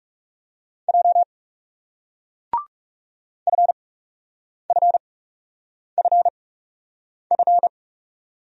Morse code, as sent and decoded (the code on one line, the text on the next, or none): .--- / ..-. ..-. ..-. ..-.
J FFFF